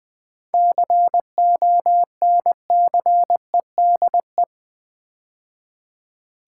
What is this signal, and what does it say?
Morse code, heard as a keyed tone, about 20 wpm, 700 Hz. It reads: CONCEDE
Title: 7 Letter Words - First 3 Letters Given - 20wpm